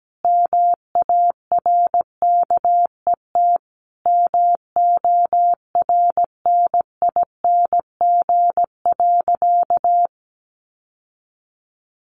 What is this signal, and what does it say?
Morse code, heard as a keyed tone, about 17 wpm, 700 Hz. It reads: MARKET MORNING.
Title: Sentences from Top 600 Words 17wpm